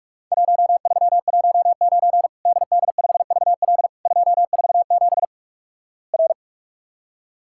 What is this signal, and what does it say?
Morse code, heard as a keyed tone, about 48 wpm, 700 Hz. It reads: 1319 DD5VL 247 R
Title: Worked All Europe DX Contest - QTC Exchange - 45wpm